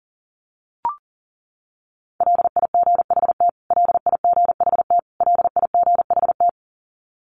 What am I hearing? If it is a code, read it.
Morse code, heard as a keyed tone, about 40 wpm, 700 Hz. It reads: LIGHT LIGHT LIGHT
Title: Top 500 Words Repeated 40wpm